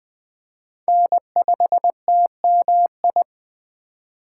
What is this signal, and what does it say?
Morse code, heard as a keyed tone, about 19 wpm, 700 Hz. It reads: N5TMI